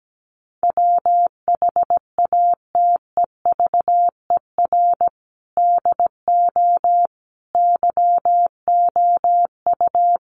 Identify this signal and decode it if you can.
Morse code, heard as a keyed tone, about 17 wpm, 700 Hz. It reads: WHATEVER DO YOU
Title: Sentences from Top 1100 Words 17wpm